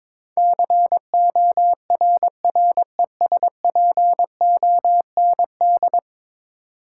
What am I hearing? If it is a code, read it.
Morse code, heard as a keyed tone, about 22 wpm, 700 Hz. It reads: CORRESPOND
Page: morse-code-ninja-practice-22wpm